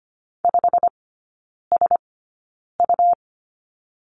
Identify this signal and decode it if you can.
Morse code, heard as a keyed tone, about 25 wpm, spaced as for 9 wpm, 700 Hz. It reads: 5SU
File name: Lesson 17e - Letter u - Three Characters 25wpm